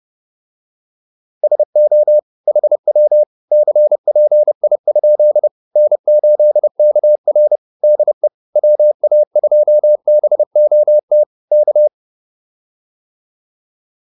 Morse code, heard as a keyed tone, about 30 wpm, 600 Hz. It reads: SO HW CPI? N8KR DE WA2BOT K
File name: QSO206 30